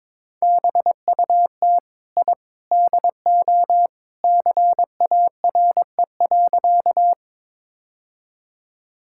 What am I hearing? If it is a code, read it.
Morse code, heard as a keyed tone, about 22 wpm, 700 Hz. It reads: BUT I DO CARE.
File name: practice / 22wpm / Sentences with 3-Word Phrases - 22wpm